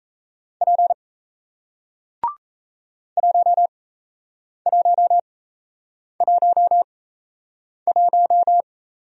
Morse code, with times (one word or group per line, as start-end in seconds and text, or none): 0.6-0.9 P
3.2-8.6 1111